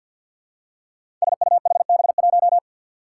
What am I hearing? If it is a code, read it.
Morse code, heard as a keyed tone, about 50 wpm, 700 Hz. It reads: SUHB1
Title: Five Challenging Letter-Number Combinations 50wpm